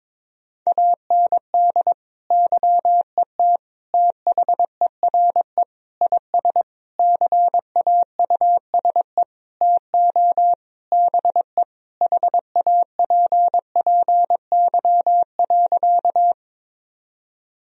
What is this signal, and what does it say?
Morse code, heard as a keyed tone, about 22 wpm, 700 Hz. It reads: AND YET THERE IS CAUSE TO BE HAPPY.
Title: Sentences from Top 1000 Words - Review Part 1 - 22wpm